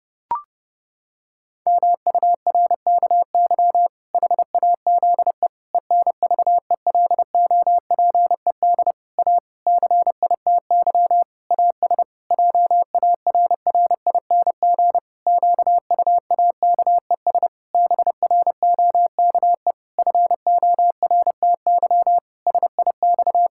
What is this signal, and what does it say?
Morse code, heard as a keyed tone, about 28 wpm, 700 Hz. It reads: MURKY HAZE ENVELOPED A CITY AS JARRING QUAKES BROKE FORTY SIX
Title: Pangram Challenge v2 - 30wpm